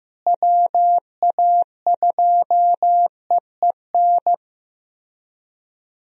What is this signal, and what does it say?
Morse code, heard as a keyed tone, about 15 wpm, 700 Hz. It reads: WA2EEN
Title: USA Call Signs - 6 Characters - Repeated 3x - 15wpm